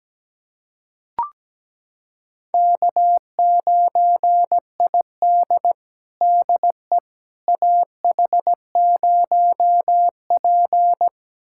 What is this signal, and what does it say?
Morse code, heard as a keyed tone, about 17 wpm, 700 Hz. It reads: K9ID DE AH0P
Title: USA Callsign DE Callsign - 4 Characters - 17wpm